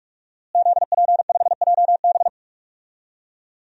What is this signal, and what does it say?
Morse code, heard as a keyed tone, about 43 wpm, 700 Hz. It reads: ZP5JB
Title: DXCC-100 Call Signs - 5 Characters - Repeated 2x - 45wpm